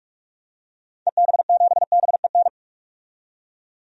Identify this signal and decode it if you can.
Morse code, heard as a keyed tone, about 46 wpm, 700 Hz. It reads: EB7BEN
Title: DXCC-100 Call Signs - 6 Characters - 45wpm